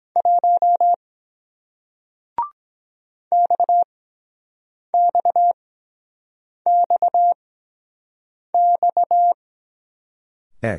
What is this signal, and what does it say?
Morse code, heard as a keyed tone, about 23 wpm, spaced as for 7 wpm, 700 Hz. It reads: 1 XXXXE